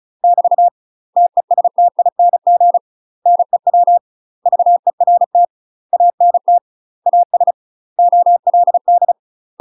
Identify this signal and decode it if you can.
Morse code, heard as a keyed tone, about 36 wpm, 700 Hz. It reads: <BT> TESTING NEW VERT ANT AS OLD
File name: UK MASTER QSOs TRAINING / 35WPM / UK MASTER QSOs 0041